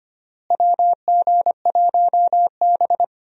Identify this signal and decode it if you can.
Morse code, heard as a keyed tone, about 25 wpm, 700 Hz. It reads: WG1B
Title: USA Call Signs - 4 Characters - Repeated 2x - 25wpm